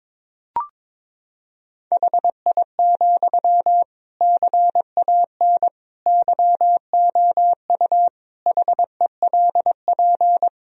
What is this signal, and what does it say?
Morse code, heard as a keyed tone, about 22 wpm, 700 Hz. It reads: HI, CAN YOU HELP
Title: Sentences from Top 1000 Words - Review Part 3 - 22wpm